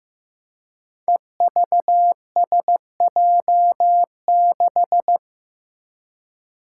Morse code, heard as a keyed tone, about 15 wpm, 700 Hz. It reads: EVSJ6
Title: Five Challenging Letter-Number Combinations 15wpm